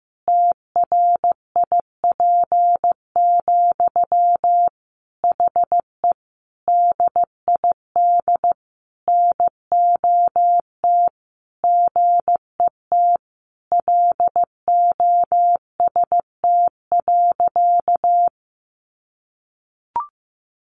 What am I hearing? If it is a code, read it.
Morse code, heard as a keyed tone, about 14 wpm, 700 Hz. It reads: TRIP, HE DID NOT GET LOST.
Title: Sentences from Top 1400 Words 15wpm